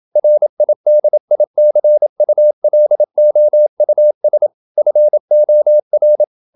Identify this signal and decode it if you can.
Morse code, heard as a keyed tone, about 27 wpm, 600 Hz. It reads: RIDICULOUS FOR
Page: Never 26wpm | CW med Gnister